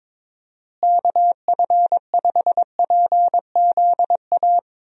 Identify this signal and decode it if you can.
Morse code, heard as a keyed tone, about 22 wpm, 700 Hz. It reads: KF5PZA
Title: USA Call Signs - 6 Characters - Repeated 3x - 22wpm